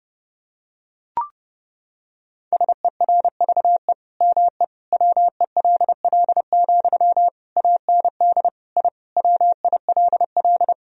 Morse code, heard as a keyed tone, about 30 wpm, 700 Hz. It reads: SERVE ME WELL, AND I WILL
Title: Sentences from Top 1000 Words - Review Part 2 - 30wpm